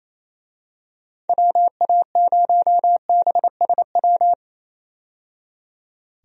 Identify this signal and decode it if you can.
Morse code, heard as a keyed tone, about 28 wpm, 700 Hz. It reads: WA0BSW
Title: USA Call Signs - 6 Characters - 28wpm